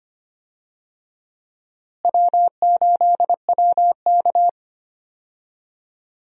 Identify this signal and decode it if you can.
Morse code, heard as a keyed tone, about 25 wpm, 700 Hz. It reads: W8WK
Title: USA Call Signs - 4 Characters - Repeated 2x - 25wpm